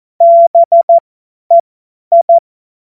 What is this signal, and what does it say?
Morse code, heard as a keyed tone, about 14 wpm, spaced as for 10 wpm, 680 Hz. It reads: BEI
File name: Morsekurs / Witze / Witz 11. Juli 2023 50 F70